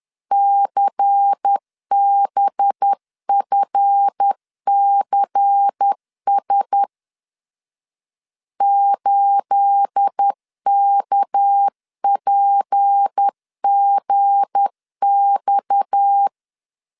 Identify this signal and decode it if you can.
Morse code, heard as a keyed tone, about 10 wpm, 800 Hz. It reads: CBFCS 8KPGX
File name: CW / DET / 1 9 1500C S A 10wpm 4511